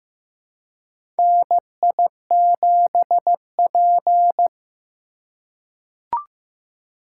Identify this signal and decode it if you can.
Morse code, heard as a keyed tone, about 15 wpm, 700 Hz. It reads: NI7P
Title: USA Call Signs - 4 Characters - 4x Character Spacing - 15wpm